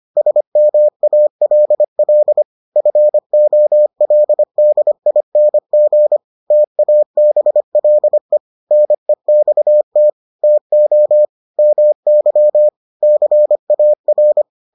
Morse code, heard as a keyed tone, about 25 wpm, 600 Hz. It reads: SMALL FOLDING TABLE NEXT TO MY CAR